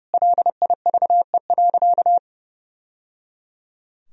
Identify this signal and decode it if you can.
Morse code, heard as a keyed tone, about 30 wpm, 700 Hz. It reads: LIVE.
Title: Sentences from Top 1000 Words - Review Part 3 - 30wpm